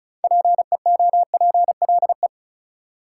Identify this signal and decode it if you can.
Morse code, heard as a keyed tone, about 36 wpm, 700 Hz. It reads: PEOPLE
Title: Sentences from Top 200 Words - New Word Drill - 35wpm